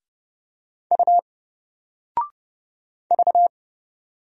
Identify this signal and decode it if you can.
Morse code, heard as a keyed tone, about 30 wpm, 700 Hz. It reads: U V
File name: Single-Letter-Number-30wpm